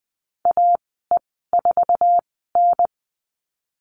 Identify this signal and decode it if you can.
Morse code, heard as a keyed tone, about 19 wpm, spaced as for 14 wpm, 700 Hz. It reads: AE4N
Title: 048 Lesson 9g - Number 4 - Call Signs 20wpm